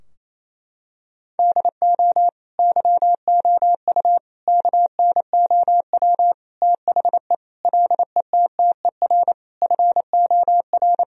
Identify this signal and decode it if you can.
Morse code, heard as a keyed tone, about 28 wpm, 700 Hz. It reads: DO YOU KNOW THE LETTER FOR
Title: Sentences from Top 1000 Words - Review Part 2 - 28wpm